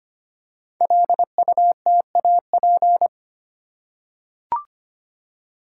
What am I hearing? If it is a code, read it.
Morse code, heard as a keyed tone, about 25 wpm, 700 Hz. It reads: LUTAP
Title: Code Groups - 5 Letters - 25wpm